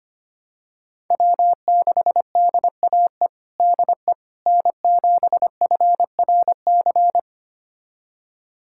Morse code, heard as a keyed tone, about 25 wpm, 700 Hz. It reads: W6DAE DE N7FRC
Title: USA Callsign DE Callsign - 5 Characters - 25wpm